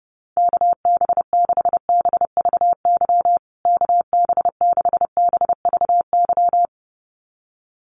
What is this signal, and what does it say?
Morse code, heard as a keyed tone, about 30 wpm, 700 Hz. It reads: KB6BVY KB6BVY
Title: USA Call Signs Repeated 2x 30wpm